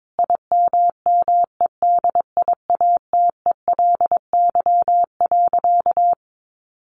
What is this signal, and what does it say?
Morse code, heard as a keyed tone, about 22 wpm, 700 Hz. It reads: IMMEDIATELY.
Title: Sentences from Top 1100 Words 22wpm